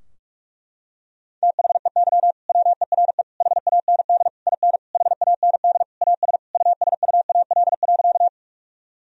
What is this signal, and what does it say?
Morse code, heard as a keyed tone, about 43 wpm, 700 Hz. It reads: THEY WERE HAND IN HAND AS USUAL.
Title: Sentences from Top 1000 Words - Review Part 2 - 45wpm